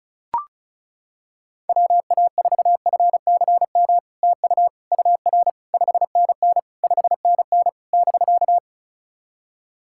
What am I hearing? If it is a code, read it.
Morse code, heard as a keyed tone, about 36 wpm, 700 Hz. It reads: WA4FCM TU UR 5NN 5NN <BK>
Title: SOTA QSO Practice 35wpm